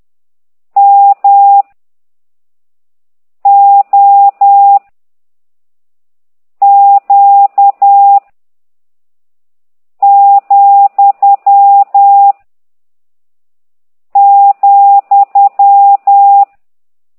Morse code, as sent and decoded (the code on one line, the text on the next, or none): -- / --- / --.- / --..-- / --..--
M O Q , ,